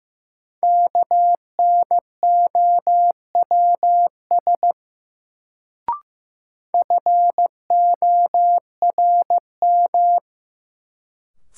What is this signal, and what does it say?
Morse code, heard as a keyed tone, about 15 wpm, 700 Hz. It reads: KNOWS FORM
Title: Sentences from Top 1000 Words - Review Part 1 - New Word Drill - 15wpm